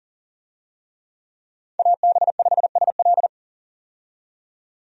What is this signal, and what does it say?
Morse code, heard as a keyed tone, about 40 wpm, 700 Hz. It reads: AB5SL
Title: USA Call Signs - 5 Characters - 40wpm